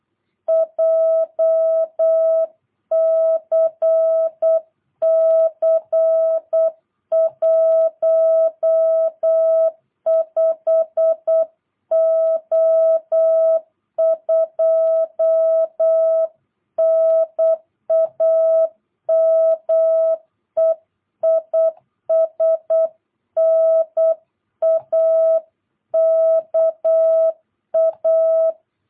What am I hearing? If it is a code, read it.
Morse code, heard as a keyed tone, about 8 wpm, 650 Hz. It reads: JCC 15O2 NAME IS NAKA
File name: jh8cbh / 2022cw